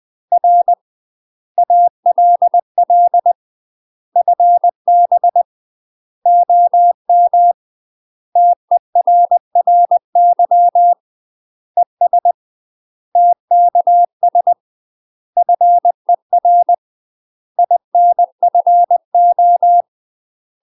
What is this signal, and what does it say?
Morse code, heard as a keyed tone, about 20 wpm, 700 Hz. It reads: R ALL FB OM TERRY ES TKS FER INFO
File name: UK MASTER QSOs TRAINING / 20WPM / UK MASTER QSOs 0024